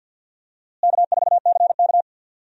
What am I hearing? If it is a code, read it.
Morse code, heard as a keyed tone, about 49 wpm, 700 Hz. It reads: K4CK
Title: USA Call Signs - 4 Characters - Repeated 2x - 50wpm